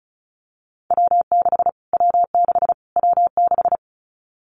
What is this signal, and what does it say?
Morse code, heard as a keyed tone, about 34 wpm, 700 Hz. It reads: W6 W6 W6
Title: Callsign Prefixes for Top 25 DXCC - Repeated 3x - 35wpm